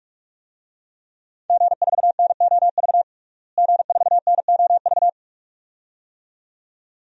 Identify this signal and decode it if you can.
Morse code, heard as a keyed tone, about 42 wpm, 700 Hz. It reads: G4NOV G4NOV
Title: DXCC-100 Call Signs - 5 Characters - Repeated 2x - 45wpm